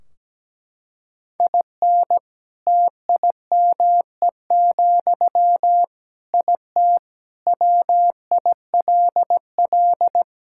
Morse code, heard as a keyed tone, about 17 wpm, 700 Hz. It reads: IN TIME, IT WILL
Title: Sentences from Top 1600 Words 17wpm